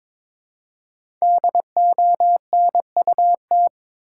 Morse code, heard as a keyed tone, about 22 wpm, 700 Hz. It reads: DONUT
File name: Difficult Words with Hints - 22wpm